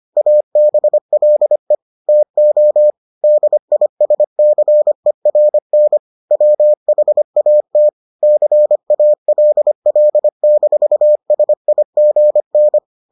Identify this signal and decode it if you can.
Morse code, heard as a keyed tone, about 25 wpm, 600 Hz. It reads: ABLE TO DISCERN WHAT CALL-SIGN